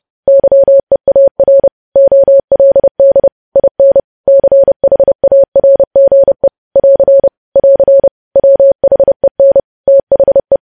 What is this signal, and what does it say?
Morse code, heard as a keyed tone, about 30 wpm, 550 Hz. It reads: YEAR OLD IN CHARGE <AR> <AR> WHEN THE
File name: podcast / News headlines-1x-30x30